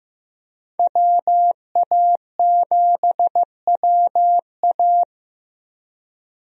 Morse code, heard as a keyed tone, about 15 wpm, 700 Hz. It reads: WA7WA
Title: USA Call Signs - 5 Characters - 15wpm